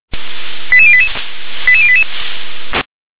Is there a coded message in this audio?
Yes: five-tone selective call (ZVEI) from 0.7 to 2.0 s, heard as quick runs of five tones, tones of about 70 ms. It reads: 88088 88088